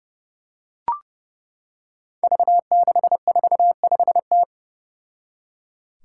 Morse code, heard as a keyed tone, about 30 wpm, 700 Hz. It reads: V645T
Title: Five Challenging Letter-Number Combinations 30wpm